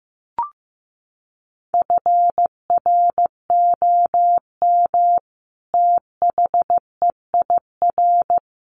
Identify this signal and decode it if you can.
Morse code, heard as a keyed tone, about 15 wpm, 700 Hz. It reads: FROM THEIR
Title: A Sound Of Thunder 15wpm